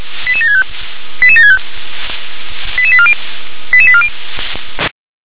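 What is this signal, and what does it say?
Five-tone selective call (ZVEI), heard as quick runs of five tones, tones of about 70 ms. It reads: 88765 88765 88840 88840
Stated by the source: ZVEI five-tone selective call 88958 received on an SDR.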